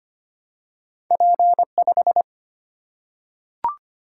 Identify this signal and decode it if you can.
Morse code, heard as a keyed tone, about 25 wpm, 700 Hz. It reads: P 5
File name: Two Letter-Number Combinations - 25wpm